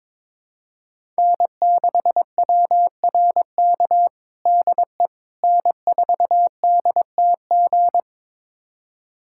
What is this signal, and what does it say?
Morse code, heard as a keyed tone, about 21 wpm, 700 Hz. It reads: N6WRK DE N4DTG
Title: USA Callsign DE Callsign 22wpm